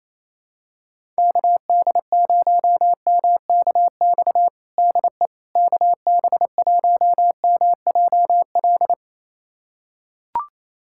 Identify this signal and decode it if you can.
Morse code, heard as a keyed tone, about 28 wpm, 700 Hz. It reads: KD0MKX DE KB1MJL
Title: USA Callsign DE Callsign - 6 Characters - 28wpm